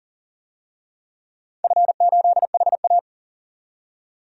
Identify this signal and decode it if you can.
Morse code, heard as a keyed tone, about 39 wpm, 700 Hz. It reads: F8HA
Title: DXCC-100 Call Signs - 4 Characters - 40wpm